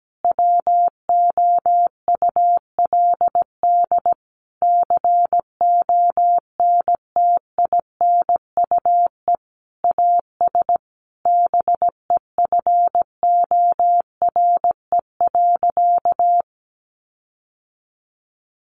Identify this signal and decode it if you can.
Morse code, heard as a keyed tone, about 17 wpm, 700 Hz. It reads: WOULD CONTINUE AS BEFORE.
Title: Sentences from Top 1400 Words 17wpm